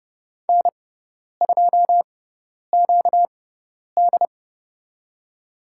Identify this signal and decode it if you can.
Morse code, heard as a keyed tone, about 30 wpm, spaced as for 10 wpm, 700 Hz. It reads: N2QD
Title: USA Call Signs - 4 Characters - 6x Character Spacing - 30wpm